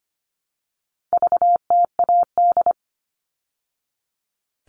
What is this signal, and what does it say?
Morse code, heard as a keyed tone, about 25 wpm, 700 Hz. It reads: VTAD
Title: All Cut Numbers - 4-Digit 25wpm